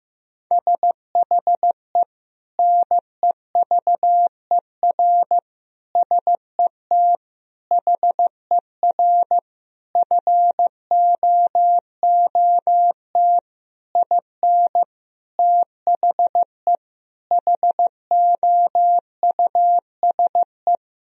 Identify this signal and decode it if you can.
Morse code, heard as a keyed tone, about 14 wpm, 700 Hz. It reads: SHE NEVER SET HER FOOT IN THE HOUSE
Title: Sentences from Top 1000 Words - Review Part 3 - 15wpm